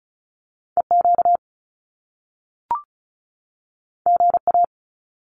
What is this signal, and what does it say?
Morse code, heard as a keyed tone, about 36 wpm, 700 Hz. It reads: EQ GA
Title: Top Two Letter Combinations in Words 35wpm